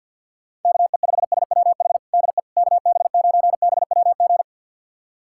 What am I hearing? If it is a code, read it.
Morse code, heard as a keyed tone, about 52 wpm, 700 Hz. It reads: KE5SWH DE KB9BWG
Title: USA Callsign DE Callsign - 6 Characters - 50wpm